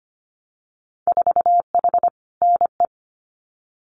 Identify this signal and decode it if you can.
Morse code, heard as a keyed tone, about 25 wpm, 690 Hz. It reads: 4H NE
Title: Winter Field Day Exchanges Encore 25wpm